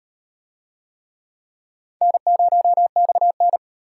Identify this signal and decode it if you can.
Morse code, heard as a keyed tone, about 38 wpm, 700 Hz. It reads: N0XN